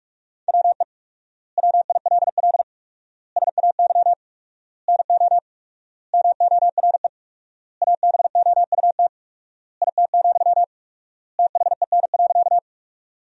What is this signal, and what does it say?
Morse code, heard as a keyed tone, about 44 wpm, 700 Hz. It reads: WE WILL SAY NO MORE ABOUT IT, THEN.
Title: 4x Word-Spacing - Sentences from Top 200 Words - 45wpm